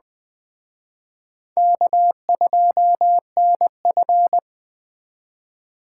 Morse code, heard as a keyed tone, about 21 wpm, 700 Hz. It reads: K2NF
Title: Speed Racing - USA 1x2 Call Signs 17wpm